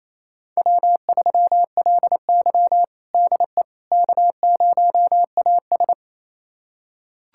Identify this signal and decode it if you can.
Morse code, heard as a keyed tone, about 28 wpm, 700 Hz. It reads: W3LY DE K0AS
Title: USA Callsign DE Callsign - 4 Characters - 28wpm